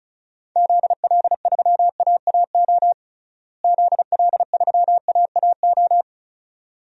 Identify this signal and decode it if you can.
Morse code, heard as a keyed tone, about 36 wpm, 700 Hz. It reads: ZL3AAO ZL3AAO
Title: DXCC-100 Call Signs - 6 Characters - Repeated 2x - 35wpm